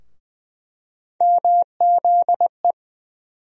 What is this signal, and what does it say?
Morse code, heard as a keyed tone, about 20 wpm, 700 Hz. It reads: MZE
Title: USA Call Sign Postfixes - 3 Letters 20wpm